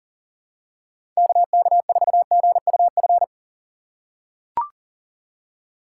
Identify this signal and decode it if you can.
Morse code, heard as a keyed tone, about 40 wpm, 700 Hz. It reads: KK4GUF
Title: USA Call Signs - 6 Characters - 6x Character Spacing - 40wpm